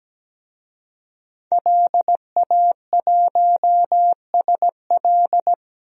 Morse code, heard as a keyed tone, about 17 wpm, 700 Hz. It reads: LA1SL
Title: DXCC-100 Call Signs - 5 Characters - 17wpm